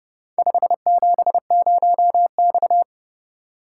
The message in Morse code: 570X

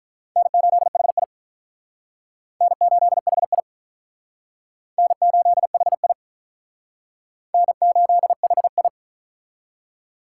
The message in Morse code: N8HI N8HI N8HI N8HI

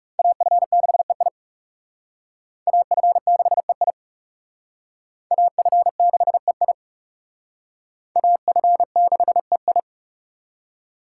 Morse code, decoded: AF6EI AF6EI AF6EI AF6EI

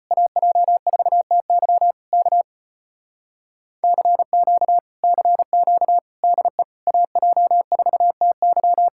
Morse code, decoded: AJ4TY K CQ CQ DE AJ4TY

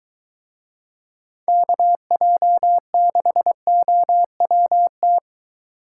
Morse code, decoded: KJ6OWT

